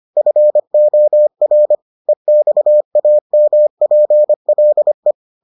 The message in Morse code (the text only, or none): FOR EXAMPLE